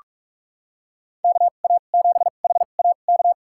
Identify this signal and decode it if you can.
Morse code, heard as a keyed tone, about 46 wpm, spaced as for 33 wpm, 700 Hz. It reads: KA7HAK
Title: USA Call Signs - 6 Characters - 2x Character Spacing - 45wpm